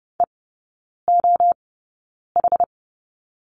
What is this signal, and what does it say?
Morse code, heard as a keyed tone, about 30 wpm, spaced as for 9 wpm, 700 Hz. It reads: EOH
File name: Lesson 18e - Letter m - Three Characters 30wpm